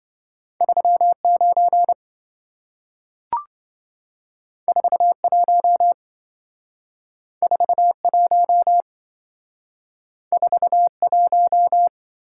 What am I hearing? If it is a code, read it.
Morse code, heard as a keyed tone, about 27 wpm, 700 Hz. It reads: 39 41 41 41